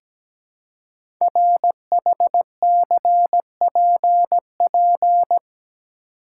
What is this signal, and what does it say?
Morse code, heard as a keyed tone, about 17 wpm, 700 Hz. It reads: RHCPP
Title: Code Groups - 5 Letters - 17wpm